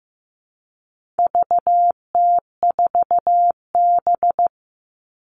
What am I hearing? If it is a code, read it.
Morse code, heard as a keyed tone, about 15 wpm, 700 Hz. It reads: VT4B
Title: All Cut Numbers - 4-Digit 15wpm